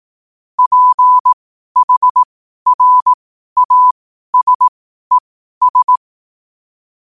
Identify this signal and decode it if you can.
Morse code, heard as a keyed tone, about 18 wpm, spaced as for 12 wpm, 1000 Hz. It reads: PHRASES
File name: qotdfiles / 12wpm